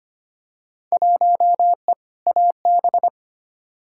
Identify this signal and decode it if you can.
Morse code, heard as a keyed tone, about 25 wpm, 700 Hz. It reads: 1E AB
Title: Field Day Exchanges - 25wpm